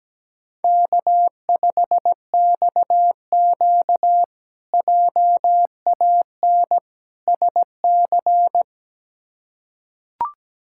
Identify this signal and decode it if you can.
Morse code, heard as a keyed tone, about 17 wpm, 700 Hz. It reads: K5XQ JAN SC
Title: K1USN SST QSO Practice - 17wpm